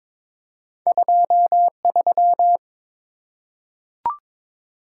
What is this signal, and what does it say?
Morse code, heard as a keyed tone, about 22 wpm, 700 Hz. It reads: 23
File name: Two Letter-Number Combinations - 22wpm